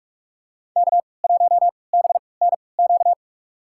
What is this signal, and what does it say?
Morse code, heard as a keyed tone, about 46 wpm, spaced as for 25 wpm, 700 Hz. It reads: K1BNQ
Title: USA Call Signs - 5 Characters - 3x Character Spacing - 45wpm